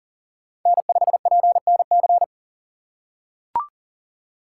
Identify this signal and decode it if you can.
Morse code, heard as a keyed tone, about 37 wpm, 700 Hz. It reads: N5PNC
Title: USA Call Signs - 3x Character Spacing - 40wpm